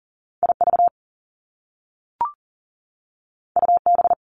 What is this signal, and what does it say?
Morse code, heard as a keyed tone, about 42 wpm, 700 Hz. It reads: IV UB
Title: Top Two Letter Combinations in Words 40wpm